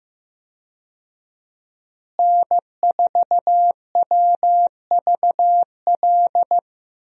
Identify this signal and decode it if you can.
Morse code, heard as a keyed tone, about 15 wpm, 700 Hz. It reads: N4WVL
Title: USA Call Signs - 5 Characters - Repeated 2x - 15wpm